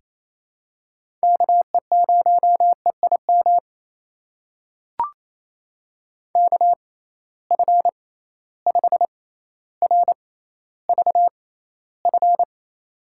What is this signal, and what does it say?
Morse code, heard as a keyed tone, about 28 wpm, spaced as for 14 wpm, 700 Hz. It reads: KE0EIM KF5RVF
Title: USA Call Signs - 6 Characters - 6x Character Spacing - 28wpm